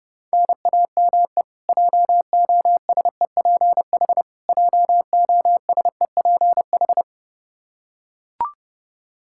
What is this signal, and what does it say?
Morse code, heard as a keyed tone, about 32 wpm, 700 Hz. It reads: NAME JOSEPH JOSEPH
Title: Practice QSO - Exchange 1 - Beginning - 30wpm